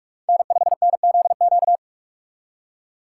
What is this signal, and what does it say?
Morse code, heard as a keyed tone, about 46 wpm, 700 Hz. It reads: N5NZQ
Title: USA Call Signs - 3x Character Spacing - 45wpm